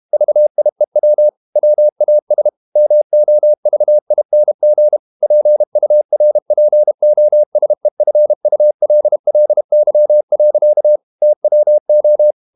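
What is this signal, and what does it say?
Morse code, heard as a keyed tone, about 32 wpm, 600 Hz. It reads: VIEW WAS MOVING PURPOSEFULLY. TWO
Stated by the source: Morse code practice at 32 wpm from CW med Gnister, reading Ken Follett's novel Never in English.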